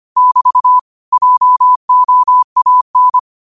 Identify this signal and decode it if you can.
Morse code, heard as a keyed tone, about 25 wpm, 1000 Hz. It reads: <BT> JOAN
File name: qotdfiles / 25wpm